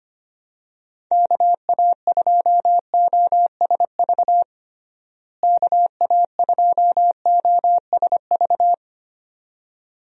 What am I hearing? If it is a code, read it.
Morse code, heard as a keyed tone, about 25 wpm, 700 Hz. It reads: KA2OSV KA2OSV
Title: USA Call Signs - 6 Characters - Repeated 2x - 25wpm